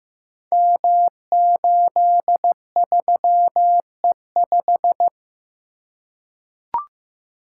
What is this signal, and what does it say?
Morse code, heard as a keyed tone, about 15 wpm, 700 Hz. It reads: M83E5
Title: Code Groups - 5 Letter:Numbers - 15wpm